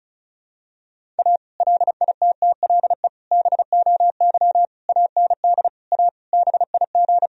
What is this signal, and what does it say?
Morse code, heard as a keyed tone, about 34 wpm, 700 Hz. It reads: A LITTLE BOY AND A BIG
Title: Sentences from Top 1000 Words - Review Part 1 - 35wpm